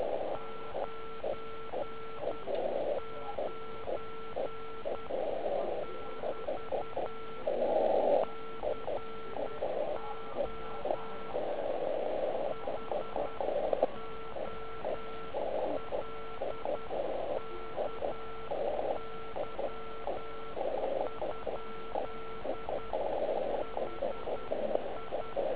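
Morse code, heard as a keyed tone, about 10 wpm, 1340 Hz. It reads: 99 <BT> CO HOLKY? HI
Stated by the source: Slow (QRS) CW test by beginner OK1IFW during beginner Morse skeds on 3559.5 kHz (80 m).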